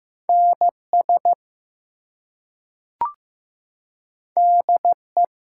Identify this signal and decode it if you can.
Morse code, heard as a keyed tone, about 15 wpm, 700 Hz. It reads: NS DE
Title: US States and Canadian Provinces 15wpm